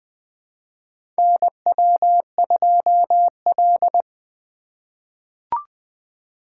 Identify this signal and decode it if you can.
Morse code, heard as a keyed tone, about 20 wpm, 700 Hz. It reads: NW2L
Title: USA Call Signs - 4 Characters - 3x Character Spacing - 20wpm